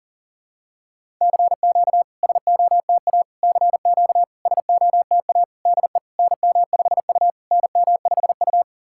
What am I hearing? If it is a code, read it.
Morse code, heard as a keyed tone, about 40 wpm, 690 Hz. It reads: CQ SOTA CQ SOTA DE NM5U NM5U
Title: SOTA QSO Practice 40wpm